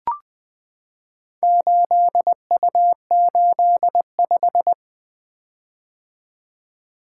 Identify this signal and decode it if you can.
Morse code, heard as a keyed tone, about 20 wpm, 700 Hz. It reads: 8U85
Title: Four Letter-Number Combinations - 20wpm